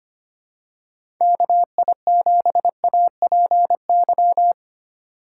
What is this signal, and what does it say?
Morse code, heard as a keyed tone, about 25 wpm, 700 Hz. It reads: KI7APY
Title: USA Call Signs - 6 Characters - 3x Character Spacing - 25wpm